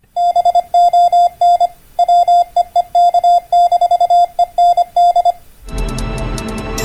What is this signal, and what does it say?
Morse code, heard as a keyed tone, about 25 wpm, 700 Hz. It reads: BON WEEK-END